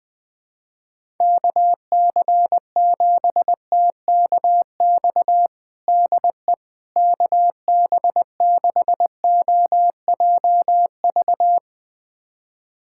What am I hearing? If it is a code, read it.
Morse code, heard as a keyed tone, about 20 wpm, 700 Hz. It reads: KC7TKX DE KB6OJV
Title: USA Callsign DE Callsign - 6 Characters - 20wpm